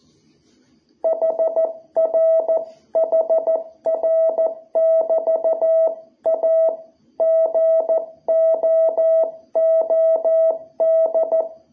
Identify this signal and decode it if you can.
Morse code, heard as a keyed tone, about 14 wpm, 660 Hz. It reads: HR HR <BT> A GOOD
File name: jh8cbh / 2021cw